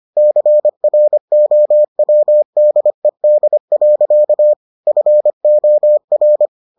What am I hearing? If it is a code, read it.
Morse code, heard as a keyed tone, about 25 wpm, 600 Hz. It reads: CROWDED. FOR